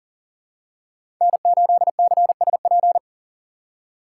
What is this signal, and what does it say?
Morse code, heard as a keyed tone, about 38 wpm, 700 Hz. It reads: N8CSP
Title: USA Call Signs - 5 Characters - 2x Character Spacing - 40wpm